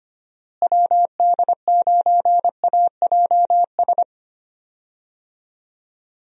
Morse code, heard as a keyed tone, about 25 wpm, 700 Hz. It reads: WD9AJS
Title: USA Call Signs - 6 Characters - Repeated 3x - 25wpm